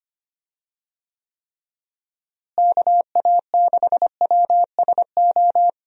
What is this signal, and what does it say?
Morse code, heard as a keyed tone, about 25 wpm, 700 Hz. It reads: KA6WSO
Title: USA Call Signs - 6 Characters - Repeated 2x - 25wpm